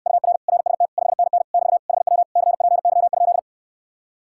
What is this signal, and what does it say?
Morse code, heard as a keyed tone, about 69 wpm, 700 Hz. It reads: UR RST 5NN <BT> HW CPY?